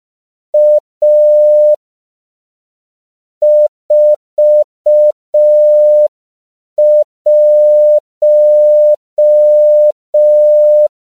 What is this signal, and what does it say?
Morse code, heard as a keyed tone, about 5 wpm, 600 Hz. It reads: A 41
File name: CW / 5PPM